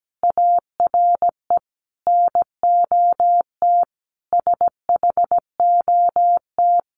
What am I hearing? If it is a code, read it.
Morse code, heard as a keyed tone, about 17 wpm, 700 Hz. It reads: ARE NOT SHOT
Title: Sentences from Top 1100 Words 17wpm